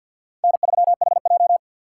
Morse code, heard as a keyed tone, about 46 wpm, 700 Hz. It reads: N3HJ
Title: USA Call Signs - 4x Character Spacing - 50wpm